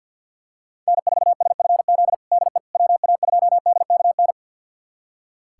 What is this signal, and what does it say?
Morse code, heard as a keyed tone, about 45 wpm, 700 Hz. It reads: N4SFZ DE WA2DKN
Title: USA Callsign DE Callsign 50wpm